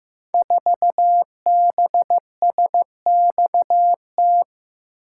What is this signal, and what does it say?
Morse code, heard as a keyed tone, about 15 wpm, 700 Hz. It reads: 4BSXT
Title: Code Groups - 5 Letter:Numbers - 15wpm